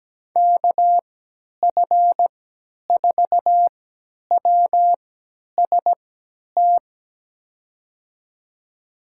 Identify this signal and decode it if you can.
Morse code, heard as a keyed tone, about 17 wpm, spaced as for 10 wpm, 700 Hz. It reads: KF4WST
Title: USA Call Signs - 6 Characters - 3x Character Spacing - 17wpm